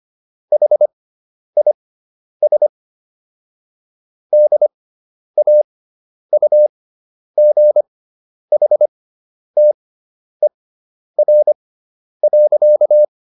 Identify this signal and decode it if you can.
Morse code, heard as a keyed tone, about 25 wpm, spaced as for 10 wpm, 600 Hz. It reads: HIS DAUGHTER.